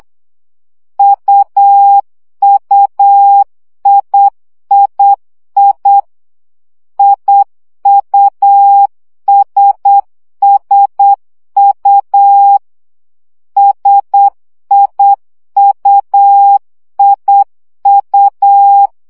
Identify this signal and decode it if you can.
Morse code, heard as a keyed tone, about 8 wpm, 800 Hz. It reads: UUIII IUSSU SIUIU
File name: CW gyakorlas i s u 7 baud